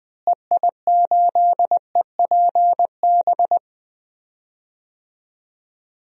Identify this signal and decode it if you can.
Morse code, heard as a keyed tone, about 20 wpm, 700 Hz. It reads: EI8EPB
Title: DXCC-100 Call Signs - 6 Characters - Repeated 2x - 20wpm